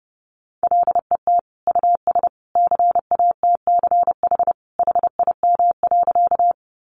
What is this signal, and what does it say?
Morse code, heard as a keyed tone, about 30 wpm, 700 Hz. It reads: LET US CATCH HIM.
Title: Sentences from Top 1500 Words 30wpm